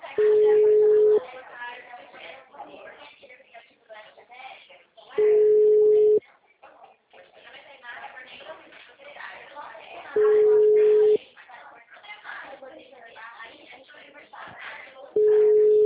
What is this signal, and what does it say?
Ringback tone (Europe, 425 Hz), with speech.
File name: out- -2003- - - .43823